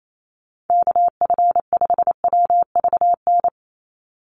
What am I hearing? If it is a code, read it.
Morse code, heard as a keyed tone, about 28 wpm, 700 Hz. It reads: KF5WVN
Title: USA Call Signs 28wpm